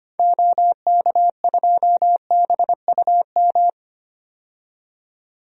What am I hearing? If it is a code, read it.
Morse code, heard as a keyed tone, about 25 wpm, 700 Hz. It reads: OK2BUM
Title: DXCC-100 Call Signs - 6 Characters - 25wpm